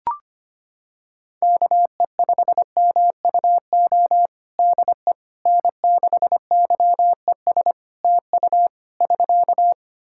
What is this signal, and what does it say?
Morse code, heard as a keyed tone, about 25 wpm, 700 Hz. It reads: KE5MUO DE N6YES TU <SK>
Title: K1USN SST QSO Practice 25wpm